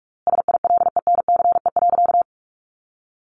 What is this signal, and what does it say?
Morse code, heard as a keyed tone, about 44 wpm, 700 Hz. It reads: SILENCE.
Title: A Sound Of Thunder 45wpm